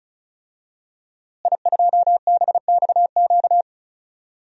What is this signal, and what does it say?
Morse code, heard as a keyed tone, about 36 wpm, 700 Hz. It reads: I2BXQ
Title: DXCC-100 Call Signs - 5 Characters - Repeated 3x - 35wpm